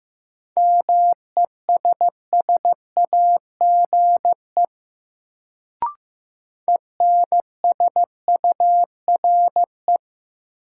Morse code, heard as a keyed tone, about 16 wpm, 700 Hz. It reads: MESSAGE ENSURE
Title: Sentences from Top 1100 Words - New Word Drill - 15wpm